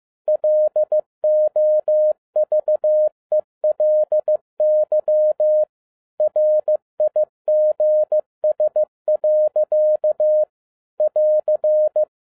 Morse code, heard as a keyed tone, about 15 wpm, 610 Hz. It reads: LOVELY RIGS. <AR>
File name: 2011-07-mp3 / n22 QRP OTAKU 15wpm